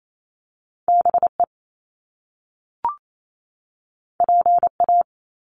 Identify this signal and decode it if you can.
Morse code, heard as a keyed tone, about 28 wpm, 700 Hz. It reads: BE PA
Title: Top Two Letter Combinations in Words 28wpm